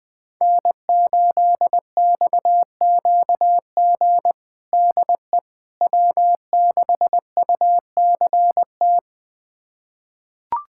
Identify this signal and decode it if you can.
Morse code, heard as a keyed tone, about 20 wpm, 700 Hz. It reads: N8XQG DE W6UCT
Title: USA Callsign DE Callsign - 5 Characters - 20wpm